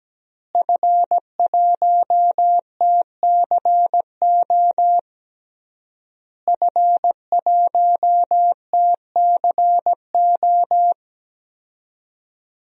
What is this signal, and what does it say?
Morse code, heard as a keyed tone, about 17 wpm, 700 Hz. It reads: F1TCO F1TCO
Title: DXCC-100 Call Signs - 5 Characters - Repeated 3x - 17wpm